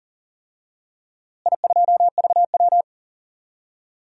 Morse code, heard as a keyed tone, about 40 wpm, 700 Hz. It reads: I2VW
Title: DXCC-100 Call Signs - 4 Characters - Repeated 3x - 40wpm